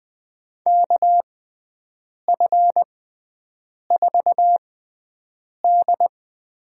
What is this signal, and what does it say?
Morse code, heard as a keyed tone, about 20 wpm, spaced as for 7 wpm, 700 Hz. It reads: KF4D
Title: USA Call Signs - 6 Characters - 6x Character Spacing - 20wpm